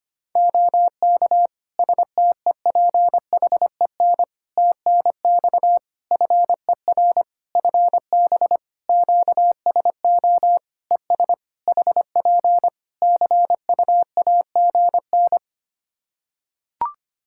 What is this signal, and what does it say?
Morse code, heard as a keyed tone, about 25 wpm, 700 Hz. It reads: OK STEPHEN TNX FER FB QSO ES HP CUAGN
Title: K1USN SST QSO Practice 25wpm